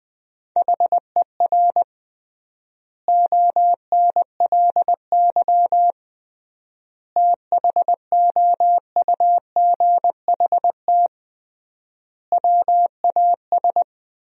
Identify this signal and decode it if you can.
Morse code, heard as a keyed tone, about 20 wpm, 700 Hz. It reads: HER ONLY THOUGHT WAS
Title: 3x Word-Spacing - Sentences from Top 400 Words - 20wpm